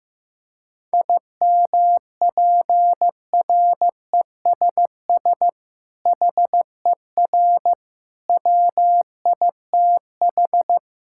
Morse code, heard as a keyed tone, about 16 wpm, 700 Hz. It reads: IMPRESS HER WITH
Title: Sentences from Top 1600 Words 15wpm